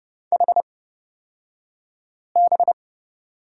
Morse code, heard as a keyed tone, about 30 wpm, spaced as for 5 wpm, 700 Hz. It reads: HB
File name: Lesson 37d - Letter x - Two Characters 30wpm